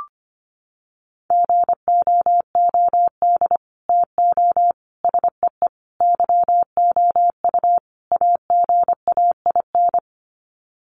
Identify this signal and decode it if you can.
Morse code, heard as a keyed tone, about 25 wpm, 700 Hz. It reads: GOOD TO SEE YOU AGAIN
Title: Sets of 5 Words Encore2 25wpm